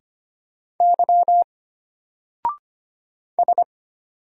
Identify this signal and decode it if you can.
Morse code, heard as a keyed tone, about 25 wpm, 700 Hz. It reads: Y S
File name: Single-Letter-Number-25wpm